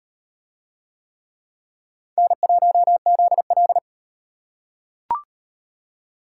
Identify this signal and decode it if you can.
Morse code, heard as a keyed tone, about 38 wpm, 700 Hz. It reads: N1ZL